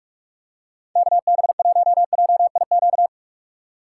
Morse code, heard as a keyed tone, about 46 wpm, 700 Hz. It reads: KB1JIQ
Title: USA Call Signs - 4x Character Spacing - 45wpm